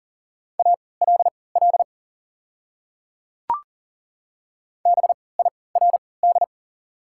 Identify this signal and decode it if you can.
Morse code, heard as a keyed tone, about 40 wpm, spaced as for 23 wpm, 690 Hz. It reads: ALL BIRD
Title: 3x Character Spacing - Top 500 Words - 40wpm